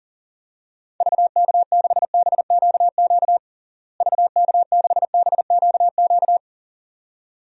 Morse code, heard as a keyed tone, about 40 wpm, 700 Hz. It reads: VK6BQQ VK6BQQ
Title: DXCC-100 Call Signs - 6 Characters - Repeated 2x - 40wpm